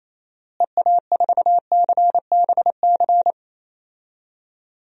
Morse code, heard as a keyed tone, about 28 wpm, 700 Hz. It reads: EA4CBC
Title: DXCC-100 Call Signs - 6 Characters - Repeated 2x - 28wpm